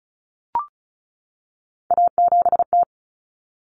Morse code, heard as a keyed tone, about 34 wpm, 700 Hz. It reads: A7T
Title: Common Cut Numbers - 3-Digit 35wpm